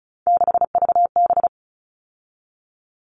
Morse code, heard as a keyed tone, about 34 wpm, 700 Hz. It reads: 6VB
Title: All Cut Numbers - 3-Digit 35wpm